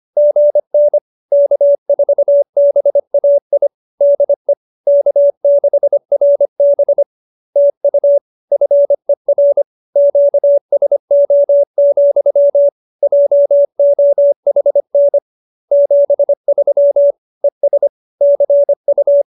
Morse code, heard as a keyed tone, about 25 wpm, 570 Hz. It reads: GN K4BAI DE K6RB TU FER QSO, JOHN 73 ES CU
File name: QSO301 25